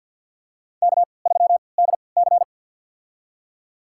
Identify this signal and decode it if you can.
Morse code, heard as a keyed tone, about 53 wpm, spaced as for 29 wpm, 700 Hz. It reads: K3DC
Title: USA Call Signs - 4 Characters - 3x Character Spacing - 50wpm